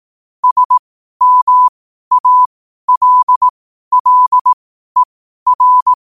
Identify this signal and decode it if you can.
Morse code, heard as a keyed tone, about 18 wpm, spaced as for 12 wpm, 1000 Hz. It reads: SMALLER
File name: qotdfiles / 12wpm